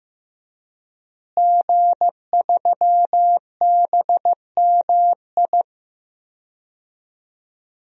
Morse code, heard as a keyed tone, about 15 wpm, 700 Hz. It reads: G3BMI
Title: DXCC-100 Call Signs - 5 Characters - Repeated 2x - 15wpm